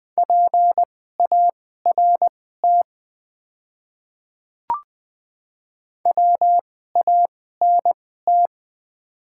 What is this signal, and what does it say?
Morse code, heard as a keyed tone, about 20 wpm, spaced as for 14 wpm, 700 Hz. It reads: PART WANT
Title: 2x Character Spacing - Top 200 Words - 20wpm